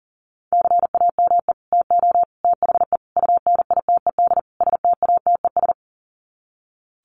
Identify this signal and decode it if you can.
Morse code, heard as a keyed tone, about 40 wpm, 700 Hz. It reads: CAME TO THE UNITED STATES
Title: Sets of 5 Words Encore2 40wpm